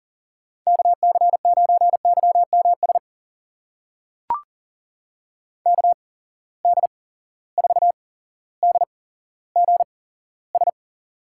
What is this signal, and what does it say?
Morse code, heard as a keyed tone, about 39 wpm, spaced as for 17 wpm, 690 Hz. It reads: KC9YMS KD4DGS
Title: USA Call Signs - 8x Character Spacing - 40wpm